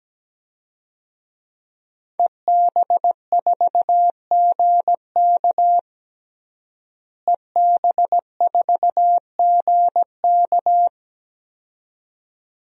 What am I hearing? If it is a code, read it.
Morse code, heard as a keyed tone, about 17 wpm, 700 Hz. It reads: EB4GK EB4GK
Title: DXCC-100 Call Signs - 5 Characters - Repeated 2x - 17wpm